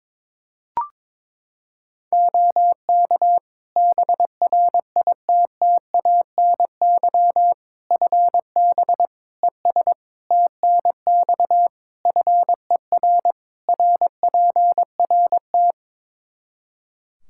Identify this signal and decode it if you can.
Morse code, heard as a keyed tone, about 22 wpm, 700 Hz. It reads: OK BRITTANY FB ES TNX FER RPRT
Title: K1USN SST QSO Practice 22wpm